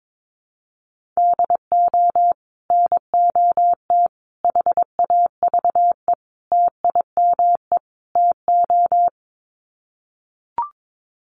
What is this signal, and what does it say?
Morse code, heard as a keyed tone, about 22 wpm, 700 Hz. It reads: DO NOT HAVE TIME TO
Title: Sets of 5 Words 22wpm